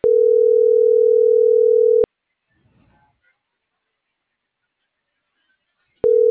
Ringback tone (North America), 440 + 480 Hz.